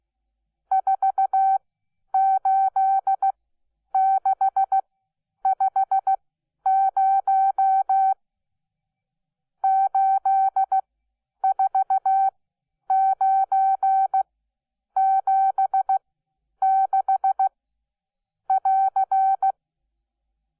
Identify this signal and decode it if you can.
Morse code, heard as a keyed tone, about 16 wpm, spaced as for 9 wpm, 770 Hz. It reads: 48650 84976<AR>
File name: Onlinelehrgang / morsen / DJ4UF Morsen12